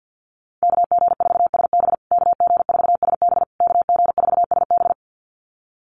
Morse code, heard as a keyed tone, about 50 wpm, 700 Hz. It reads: KG4SB KG4SB KG4SB
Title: USA Call Signs Repeated 3x 50wpm